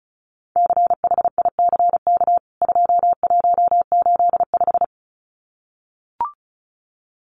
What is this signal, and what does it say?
Morse code, heard as a keyed tone, about 34 wpm, 700 Hz. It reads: CHICK 2185
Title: CWT Exchange 35wpm